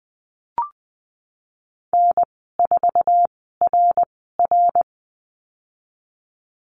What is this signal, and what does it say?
Morse code, heard as a keyed tone, about 20 wpm, spaced as for 15 wpm, 700 Hz. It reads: N4RR
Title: 055 Lesson 10g - Letter r - Call Signs 20wpm